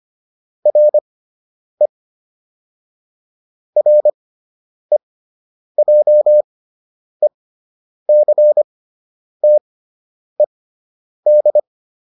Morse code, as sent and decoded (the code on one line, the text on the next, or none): .-. / .-. .--- . -.-. - . -..
RE REJECTED